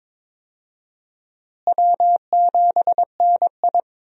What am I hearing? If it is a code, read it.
Morse code, heard as a keyed tone, about 21 wpm, 700 Hz. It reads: W7NI
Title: USA Call Signs - 4 Characters - Repeated 2x - 22wpm